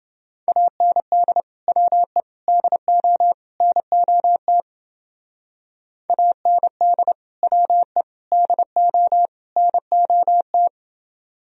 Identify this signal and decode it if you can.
Morse code, heard as a keyed tone, about 29 wpm, 700 Hz. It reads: AND WE DO NOT AND WE DO NOT